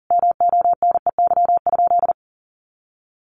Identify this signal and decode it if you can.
Morse code, heard as a keyed tone, about 40 wpm, 700 Hz. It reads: MONEY?
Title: Sentences from Top 400 Words 40wpm